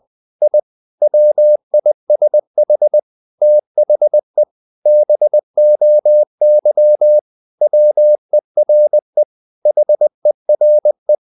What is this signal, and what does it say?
Morse code, heard as a keyed tone, about 20 wpm, 600 Hz. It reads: I WISH THE BOY WERE HERE
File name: the old man and the sea / the old man and the sea part 1 20wpm